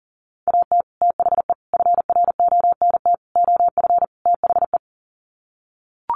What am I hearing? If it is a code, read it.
Morse code, heard as a keyed tone, about 40 wpm, 700 Hz. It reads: AT THE FRONT OF THE
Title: Sets of 5 Words 40wpm